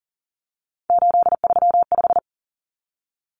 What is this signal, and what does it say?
Morse code, heard as a keyed tone, about 40 wpm, 700 Hz. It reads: 835
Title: Numbers - 3-Digit 40wpm